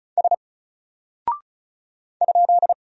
Morse code, heard as a keyed tone, about 35 wpm, 700 Hz. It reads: S ?
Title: Single Character (Letter:Number:Punctuation:Prosign) 35wpm